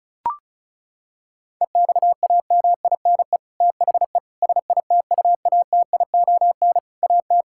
Morse code, heard as a keyed tone, about 36 wpm, 700 Hz. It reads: EXAMINE THE SITUATION AT
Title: Sentences from Top 1600 Words 35wpm